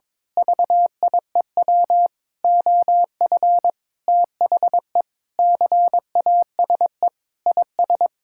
Morse code, heard as a keyed tone, about 22 wpm, 700 Hz. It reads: VIEW OF THE CASE IS
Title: Sentences from Top 1000 Words - Review Part 6 - 22wpm